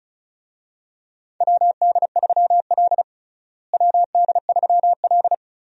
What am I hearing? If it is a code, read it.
Morse code, heard as a keyed tone, about 34 wpm, 700 Hz. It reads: WD3L WD3L
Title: USA Call Signs - 4 Characters - Repeated 2x - 35wpm